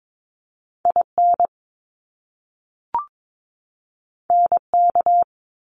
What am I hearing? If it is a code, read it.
Morse code, heard as a keyed tone, about 22 wpm, 700 Hz. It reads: IN NK
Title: Top Two Letter Combinations in Words 22wpm